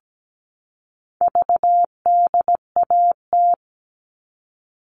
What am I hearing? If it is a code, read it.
Morse code, heard as a keyed tone, about 17 wpm, 700 Hz. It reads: VDAT
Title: All Cut Numbers - 4-Digit 17wpm